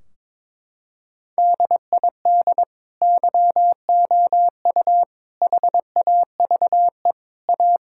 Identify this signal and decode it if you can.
Morse code, heard as a keyed tone, about 22 wpm, 700 Hz. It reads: DID YOU HAVE A
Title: Sentences from Top 1600 Words 22wpm